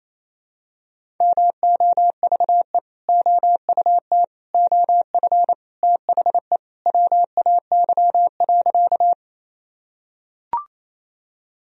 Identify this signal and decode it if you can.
Morse code, heard as a keyed tone, about 28 wpm, 700 Hz. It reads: MOVE OUT OF THE WAY.
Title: Sentences from Top 1000 Words - Review Part 1 - 28wpm